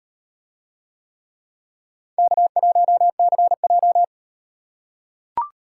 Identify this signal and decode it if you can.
Morse code, heard as a keyed tone, about 37 wpm, 700 Hz. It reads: K1CJ